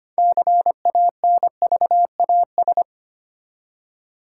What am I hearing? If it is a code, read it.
Morse code, heard as a keyed tone, about 25 wpm, 700 Hz. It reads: CANVAS
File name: practice / 25wpm / Easy Words with Hints - 25wpm